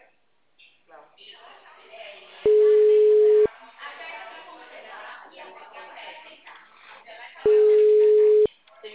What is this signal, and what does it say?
Ringback tone (Europe, 425 Hz), with speech.
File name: out- -2020